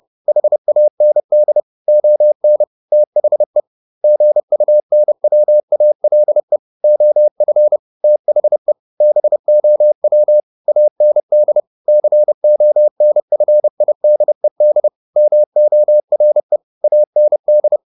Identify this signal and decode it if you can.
Morse code, heard as a keyed tone, about 30 wpm, 600 Hz. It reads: HAND ON THE GUNWALE OF THE BOW AND CONFIDED MORE AND